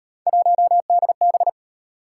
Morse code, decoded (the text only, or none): 1DB